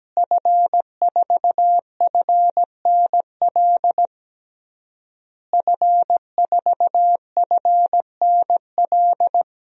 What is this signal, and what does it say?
Morse code, heard as a keyed tone, about 17 wpm, 700 Hz. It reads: F4FNL F4FNL
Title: DXCC-100 Call Signs - 5 Characters - Repeated 3x - 17wpm